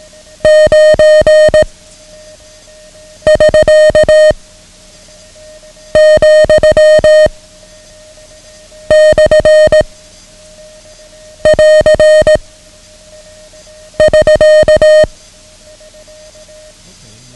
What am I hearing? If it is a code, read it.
Morse code, heard as a keyed tone, about 18 wpm, 620 Hz. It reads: 9 <SK> , / <AR> <SK>
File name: ham radio / CodeQuick / Code Quick 09